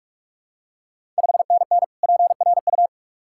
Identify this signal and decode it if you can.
Morse code, heard as a keyed tone, about 44 wpm, 700 Hz. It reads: 5NN PRU
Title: Ontario QSO Party Exchange 45wpm